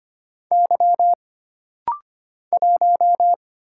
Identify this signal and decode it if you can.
Morse code, heard as a keyed tone, about 25 wpm, 700 Hz. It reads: Y 1
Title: Single Letter-Number - Rapid-Fire 25wpm